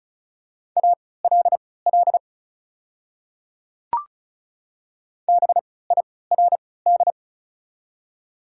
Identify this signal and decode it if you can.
Morse code, heard as a keyed tone, about 34 wpm, spaced as for 20 wpm, 690 Hz. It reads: ALL BIRD
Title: 3x Character Spacing - Top 500 Words - 35wpm